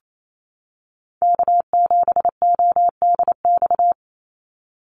Morse code, heard as a keyed tone, about 28 wpm, 700 Hz. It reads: K7ODX
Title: NA SOTA Activator Callsigns 28wpm